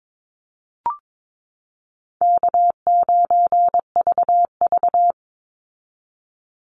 Morse code, heard as a keyed tone, about 22 wpm, 690 Hz. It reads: K9VV
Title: Call Signs Repeated 2x - 22wpm